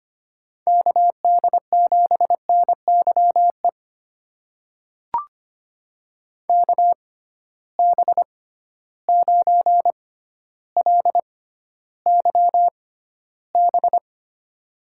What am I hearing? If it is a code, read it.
Morse code, heard as a keyed tone, about 25 wpm, spaced as for 13 wpm, 700 Hz. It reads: KD7NYE KB9LYB
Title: USA Call Signs - 6 Characters - 6x Character Spacing - 25wpm